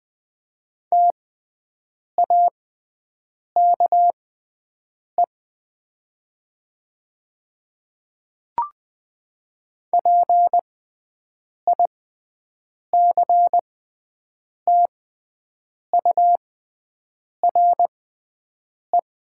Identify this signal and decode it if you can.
Morse code, heard as a keyed tone, about 20 wpm, spaced as for 7 wpm, 700 Hz. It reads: TAKE PICTURE
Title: 6x Character Spacing - Top 300 Words - 20wpm